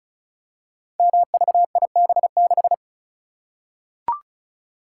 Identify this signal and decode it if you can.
Morse code, heard as a keyed tone, about 36 wpm, 690 Hz. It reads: MVIB6
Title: Code Groups - 5 Letter:Numbers - 35wpm